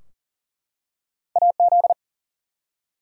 Morse code, heard as a keyed tone, about 39 wpm, 700 Hz. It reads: AZ
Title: US States and Canadian Provinces 40wpm